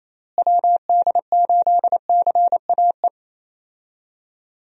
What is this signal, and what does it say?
Morse code, heard as a keyed tone, about 28 wpm, 700 Hz. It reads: WD8CAE
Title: USA Call Signs - 6 Characters - Repeated 3x - 28wpm